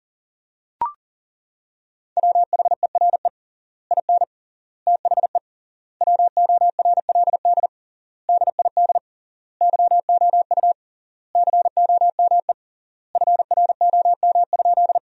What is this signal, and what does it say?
Morse code, heard as a keyed tone, about 40 wpm, 700 Hz. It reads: WHERE IN THE WORLD DID YOU COME FROM?
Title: 3x Word-Spacing - Sentences from Top 200 Words - 40wpm